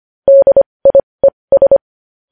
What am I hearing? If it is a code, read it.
Morse code, heard as a keyed tone, about 25 wpm, spaced as for 20 wpm, 550 Hz. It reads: DIES
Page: News Headlines in Morse Code at 20 WPM